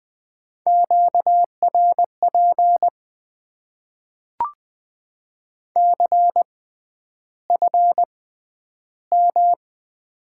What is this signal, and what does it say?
Morse code, heard as a keyed tone, about 20 wpm, spaced as for 10 wpm, 700 Hz. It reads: QRP CFM
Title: Top 90 QSO Elements - 6x Character Spacing - 20wpm